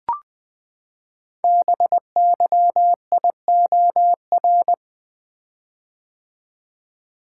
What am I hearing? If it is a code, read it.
Morse code, heard as a keyed tone, about 20 wpm, 700 Hz. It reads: BYIOR
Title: Code Groups - 5 Letters - 20wpm